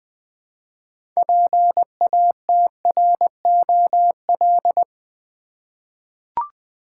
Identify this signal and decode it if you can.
Morse code, heard as a keyed tone, about 21 wpm, 700 Hz. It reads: PATROL
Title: 6 Letter Words - First 3 Letters Given - 20wpm